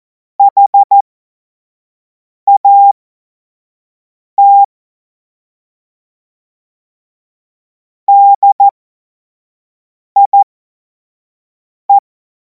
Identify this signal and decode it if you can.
Morse code, heard as a keyed tone, about 14 wpm, spaced as for 5 wpm, 800 Hz. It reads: HAT DIE
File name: Morsekurs / Witze / Witz 07. Juli 2026 25 F70